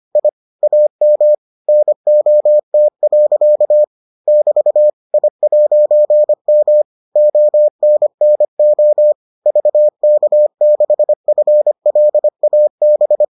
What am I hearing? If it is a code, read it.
Morse code, heard as a keyed tone, about 25 wpm, 600 Hz. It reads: I AM NOT. <BT> I'M ONNO VK6FLAB